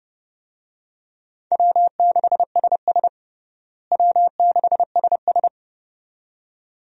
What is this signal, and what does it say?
Morse code, heard as a keyed tone, about 30 wpm, 700 Hz. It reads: W6SS W6SS
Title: USA Call Signs - 4 Characters - Repeated 2x - 30wpm